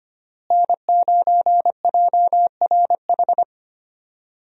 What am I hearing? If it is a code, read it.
Morse code, heard as a keyed tone, about 25 wpm, 700 Hz. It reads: N9JRH
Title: USA Call Signs - 5 Characters - Repeated 2x - 25wpm